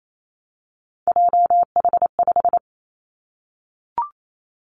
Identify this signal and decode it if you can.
Morse code, heard as a keyed tone, about 28 wpm, 700 Hz. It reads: JH5
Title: Three Challenging Letter-Number Combinations 28wpm